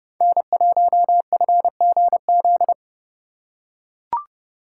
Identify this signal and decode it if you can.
Morse code, heard as a keyed tone, about 29 wpm, 700 Hz. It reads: N1FGZ